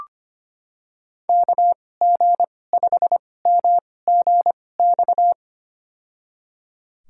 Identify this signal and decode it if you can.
Morse code, heard as a keyed tone, about 25 wpm, spaced as for 18 wpm, 700 Hz. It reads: KG5MGX
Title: USA Call Signs - 6 Characters - 2x Character Spacing - 25wpm